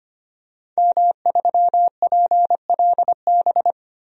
Morse code, heard as a keyed tone, about 25 wpm, 700 Hz. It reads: M3PLB
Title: DXCC-100 Call Signs - 5 Characters - 25wpm